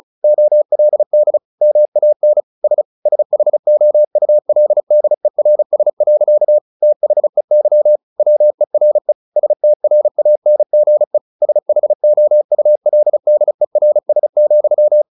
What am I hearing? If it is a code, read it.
Morse code, heard as a keyed tone, about 34 wpm, 600 Hz. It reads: OLD MAN S SHOULDERS. THEY WERE STRANGE SHOULDERS,
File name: the old man and the sea / the old man and the sea 1 35wpm